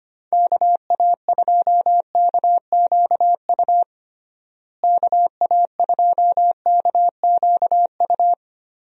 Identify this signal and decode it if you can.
Morse code, heard as a keyed tone, about 25 wpm, 700 Hz. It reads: KA2KQU KA2KQU
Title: USA Call Signs - 6 Characters - Repeated 2x - 25wpm